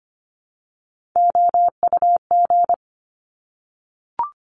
Morse code, heard as a keyed tone, about 25 wpm, 700 Hz. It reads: OUG
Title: Top Three Letter Combinations in Words 25wpm